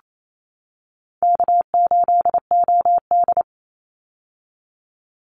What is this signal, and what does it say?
Morse code, heard as a keyed tone, about 28 wpm, 700 Hz. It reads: K8OD
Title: Easy Call Signs - 28wpm